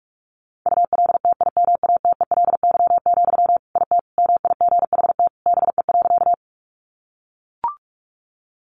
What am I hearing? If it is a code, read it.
Morse code, heard as a keyed tone, about 44 wpm, 700 Hz. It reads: ULTIMATELY, IT MIGHT BE.
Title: Sentences from Top 1200 Words 45wpm